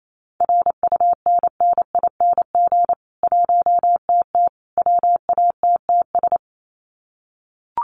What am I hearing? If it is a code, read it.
Morse code, heard as a keyed tone, about 28 wpm, 700 Hz. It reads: RUNNING 1TT WATTS
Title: Common QSO Phrases 28wpm